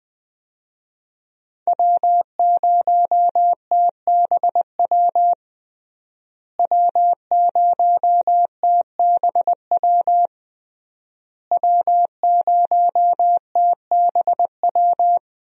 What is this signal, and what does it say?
Morse code, heard as a keyed tone, about 20 wpm, 700 Hz. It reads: W0TBW W0TBW W0TBW
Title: USA Call Signs - 5 Characters - Repeated 3x - 20wpm